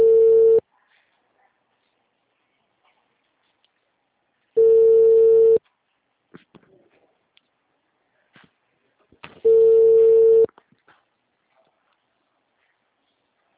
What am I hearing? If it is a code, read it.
Ringback tone (China, 450 Hz).